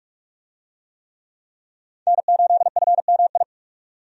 Morse code, heard as a keyed tone, about 47 wpm, 700 Hz. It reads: N8FMI
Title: USA Call Signs - 5 Characters - 45wpm